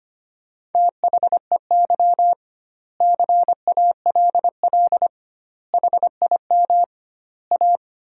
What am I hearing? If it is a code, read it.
Morse code, heard as a keyed tone, about 25 wpm, 700 Hz. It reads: THEY CALL HIM A
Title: 2x Word-Spacing - Sentences from Top 500 Words - 25wpm